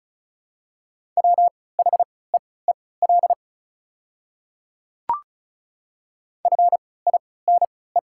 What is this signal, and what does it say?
Morse code, heard as a keyed tone, about 36 wpm, spaced as for 20 wpm, 690 Hz. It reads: WHEEL FINE
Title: 3x Character Spacing - Top 500 Words - 35wpm